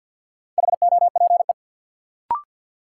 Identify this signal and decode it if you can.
Morse code, heard as a keyed tone, about 46 wpm, 700 Hz. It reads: HOPE
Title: Sentences from Top 1000 Words - New Word Drill - 50wpm